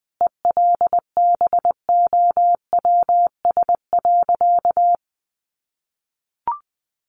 Morse code, heard as a keyed tone, about 20 wpm, 700 Hz. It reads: ELBOWS.
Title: A Sound Of Thunder 20wpm